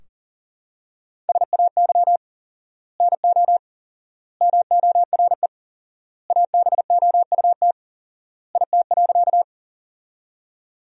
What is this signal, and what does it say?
Morse code, heard as a keyed tone, about 37 wpm, 700 Hz. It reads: SAY NO MORE ABOUT IT.
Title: 4x Word-Spacing - Sentences from Top 200 Words - 40wpm